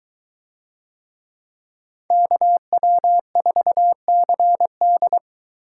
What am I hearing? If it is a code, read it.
Morse code, heard as a keyed tone, about 22 wpm, 700 Hz. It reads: KW4CD